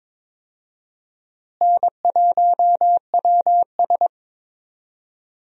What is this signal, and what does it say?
Morse code, heard as a keyed tone, about 22 wpm, 700 Hz. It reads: N1WS
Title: USA Call Signs - 4 Characters - Repeated 3x - 22wpm